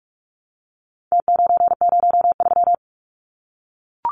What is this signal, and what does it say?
Morse code, heard as a keyed tone, about 44 wpm, 700 Hz. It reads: T903
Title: Numbers - Contest Style - 4-Digit 45wpm